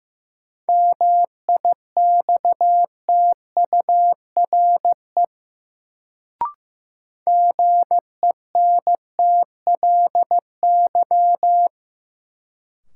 Morse code, heard as a keyed tone, about 15 wpm, 700 Hz. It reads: MIXTURE GENTLY
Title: Sentences from Top 1500 Words - New Word Drill - 15wpm